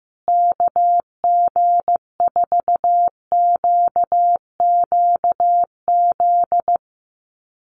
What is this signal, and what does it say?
Morse code, heard as a keyed tone, about 15 wpm, 700 Hz. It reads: KG4QQZ
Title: USA Call Signs Repeated 2x 15wpm